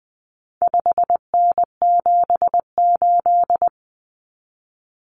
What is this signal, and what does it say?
Morse code, heard as a keyed tone, about 20 wpm, 700 Hz. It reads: 5N78
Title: Common Cut Numbers - 4-Digit 20wpm